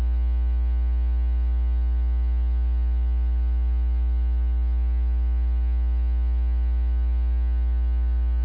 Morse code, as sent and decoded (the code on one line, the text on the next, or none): none